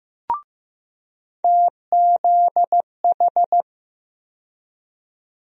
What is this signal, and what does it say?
Morse code, heard as a keyed tone, about 15 wpm, 700 Hz. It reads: TZH
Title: USA Call Sign Postfixes - 3 Letters 15wpm